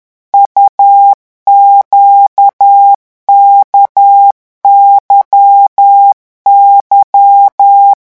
Morse code, heard as a keyed tone, about 10 wpm, 800 Hz. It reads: UQKYY
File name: morse mp3 / revision2-vitesse 10 mots